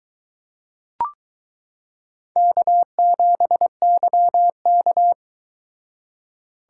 Morse code, heard as a keyed tone, about 22 wpm, 700 Hz. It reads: K7YK